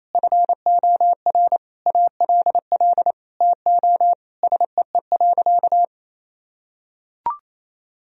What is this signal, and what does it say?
Morse code, heard as a keyed tone, about 27 wpm, 700 Hz. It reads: FOR ALL TO SEE.
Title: Sentences from Top 1000 Words - Review Part 4 - 28wpm